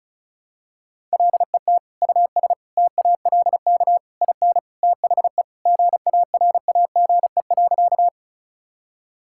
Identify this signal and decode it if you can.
Morse code, heard as a keyed tone, about 36 wpm, 700 Hz. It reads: LET US TALK IN THE GARAGE.
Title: Sentences from Top 1600 Words 35wpm